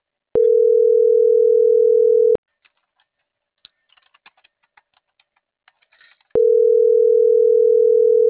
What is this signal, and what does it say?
Ringback tone (North America), 440 + 480 Hz.